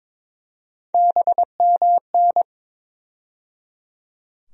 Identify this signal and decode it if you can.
Morse code, heard as a keyed tone, about 22 wpm, 700 Hz. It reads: BMN
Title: USA Call Sign Postfixes - 3 Letters 22wpm